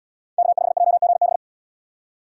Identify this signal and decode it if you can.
Morse code, heard as a keyed tone, about 67 wpm, 700 Hz. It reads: K5JGB